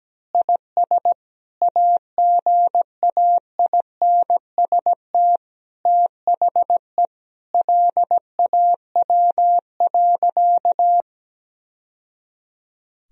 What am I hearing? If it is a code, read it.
Morse code, heard as a keyed tone, about 17 wpm, 700 Hz. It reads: IS AGAINST THE LAW.
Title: Sentences from Top 1000 Words - Review Part 3 - 17wpm